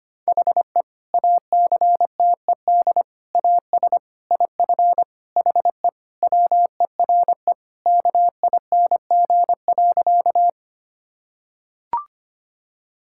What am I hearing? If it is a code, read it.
Morse code, heard as a keyed tone, about 25 wpm, 700 Hz. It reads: HE ACTED AS IF HE WERE KING.
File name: practice / 25wpm / Sentences with 4-Word Phrases - 25wpm